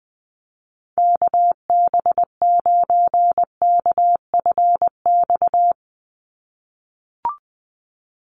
Morse code, heard as a keyed tone, about 20 wpm, 700 Hz. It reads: KB9KFX
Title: USA Call Signs 20wpm